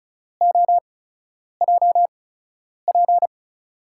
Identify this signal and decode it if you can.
Morse code, heard as a keyed tone, about 36 wpm, spaced as for 10 wpm, 700 Hz. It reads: OJP
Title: USA Call Signs - 8x Character Spacing - 35wpm